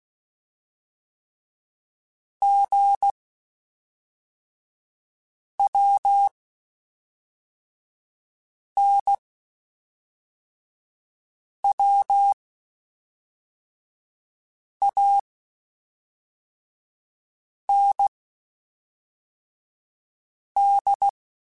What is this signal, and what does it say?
Morse code, heard as a keyed tone, about 16 wpm, 800 Hz. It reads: G W N W A N D